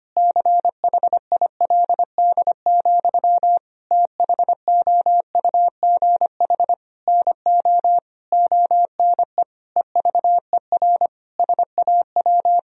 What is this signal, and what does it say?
Morse code, heard as a keyed tone, about 25 wpm, 700 Hz. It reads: CHILD, THOUGH NO ONE EVER SAW
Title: Sentences from Top 1000 Words - Review Part 2 - 25wpm